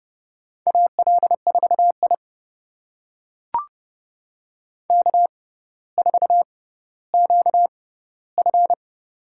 Morse code, heard as a keyed tone, about 30 wpm, spaced as for 15 wpm, 700 Hz. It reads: AL4I K4QF
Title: USA Call Signs - 4 Characters - 6x Character Spacing - 30wpm